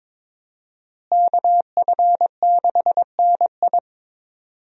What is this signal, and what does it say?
Morse code, heard as a keyed tone, about 22 wpm, 700 Hz. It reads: KF6NI
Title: USA Call Signs - 5 Characters - Repeated 3x - 22wpm